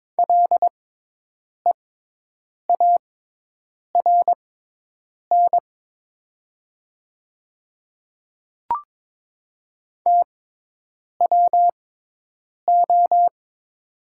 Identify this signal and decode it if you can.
Morse code, heard as a keyed tone, about 23 wpm, spaced as for 8 wpm, 700 Hz. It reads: LEARN TWO
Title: 6x Character Spacing - Top 500 Words - 22wpm